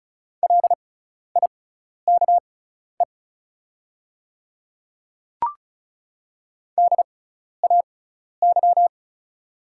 Morse code, heard as a keyed tone, about 36 wpm, spaced as for 12 wpm, 700 Hz. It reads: LIKE DAY
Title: 6x Character Spacing - Top 100 Words - 35wpm